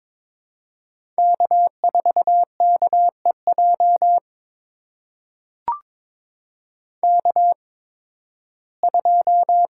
Morse code, heard as a keyed tone, about 22 wpm, spaced as for 14 wpm, 700 Hz. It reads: K4KEJ K2
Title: USA Call Signs - 5 Characters - 8x Character Spacing - 22wpm